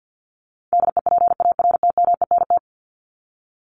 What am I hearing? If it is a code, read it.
Morse code, heard as a keyed tone, about 50 wpm, 700 Hz. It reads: DEPARTMENT
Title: Top 751-1000 Words 50wpm